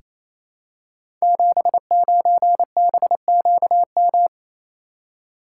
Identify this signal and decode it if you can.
Morse code, heard as a keyed tone, about 28 wpm, 700 Hz. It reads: 79BQM
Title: Code Groups - 5 Letter:Numbers - 28wpm